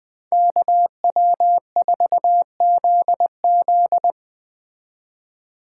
Morse code, heard as a keyed tone, about 20 wpm, 700 Hz. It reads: KW4ZZ